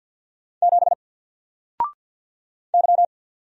Morse code, heard as a keyed tone, about 50 wpm, 700 Hz. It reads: 7 Y
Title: Single Letter-Number 50wpm